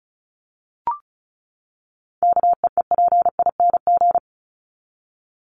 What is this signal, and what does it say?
Morse code, heard as a keyed tone, about 34 wpm, 700 Hz. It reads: KEEPING
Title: ING Suffix - Top 100 Words with Base Words 35wpm